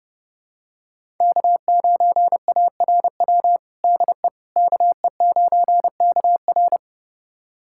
Morse code, heard as a keyed tone, about 30 wpm, 700 Hz. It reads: K9ARW DE KE9KR
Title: USA Callsign DE Callsign - 5 Characters - 30wpm